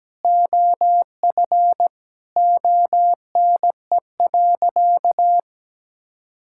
Morse code, heard as a keyed tone, about 17 wpm, 700 Hz. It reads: OF ONE.
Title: Sentences from Top 1000 Words - Review Part 5 - 17wpm